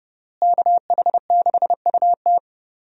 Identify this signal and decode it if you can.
Morse code, heard as a keyed tone, about 32 wpm, 700 Hz. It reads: KH6UT
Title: USA Call Signs - 5 Characters - Repeated 2x - 30wpm